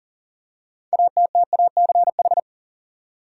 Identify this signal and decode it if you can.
Morse code, heard as a keyed tone, about 40 wpm, 700 Hz. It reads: ATTACH
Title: Six Letter Words - 40wpm